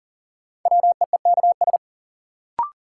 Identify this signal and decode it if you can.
Morse code, heard as a keyed tone, about 40 wpm, 700 Hz. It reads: WEEKS
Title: Sets of 5 Words - New Word Drill - 40wpm